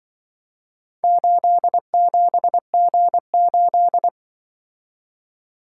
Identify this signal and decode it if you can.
Morse code, heard as a keyed tone, about 23 wpm, 700 Hz. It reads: 87G8
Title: Speed-Racing - Four Letter-Number Combinations - 20wpm